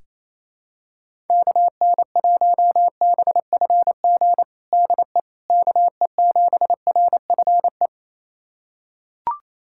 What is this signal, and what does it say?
Morse code, heard as a keyed tone, about 28 wpm, 700 Hz. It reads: KN1BFG DE KE7RFE
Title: USA Callsign DE Callsign - 6 Characters - 28wpm